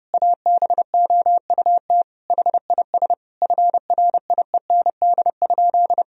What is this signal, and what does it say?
Morse code, heard as a keyed tone, about 30 wpm, 700 Hz. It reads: ABOUT HIS FRIEND?
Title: Sentences from Top 1000 Words - Review Part 3 - 30wpm